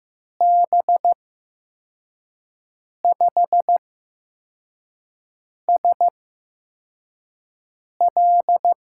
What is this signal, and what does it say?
Morse code, heard as a keyed tone, about 15 wpm, spaced as for 4 wpm, 700 Hz. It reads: B5SL
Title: USA Call Signs - 5 Characters - 8x Character Spacing - 15wpm